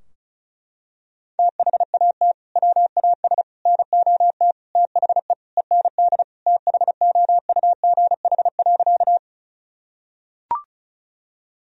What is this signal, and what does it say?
Morse code, heard as a keyed tone, about 34 wpm, 700 Hz. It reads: THAT WAS NOT THE END THOUGH.
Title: Sentences from Top 1000 Words - Review Part 2 - 35wpm